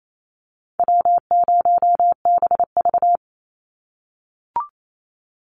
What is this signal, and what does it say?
Morse code, heard as a keyed tone, about 28 wpm, 700 Hz. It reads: W0BV
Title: Easy Call Signs - 28wpm